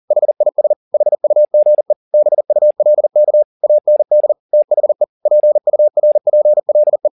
Morse code, heard as a keyed tone, about 40 wpm, 600 Hz. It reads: HIS HUGE BULK AND THE PURPLE
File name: the old man and the sea 2 40wpm